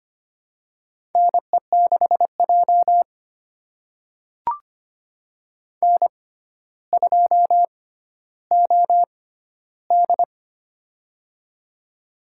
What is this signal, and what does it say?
Morse code, heard as a keyed tone, about 25 wpm, spaced as for 13 wpm, 700 Hz. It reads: NE6J N2OD
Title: USA Call Signs - 4 Characters - 6x Character Spacing - 25wpm